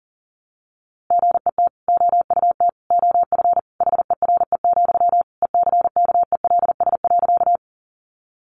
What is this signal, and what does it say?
Morse code, heard as a keyed tone, about 40 wpm, 700 Hz. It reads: GET OUT OF HERE, ECKELS.
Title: A Sound Of Thunder 40wpm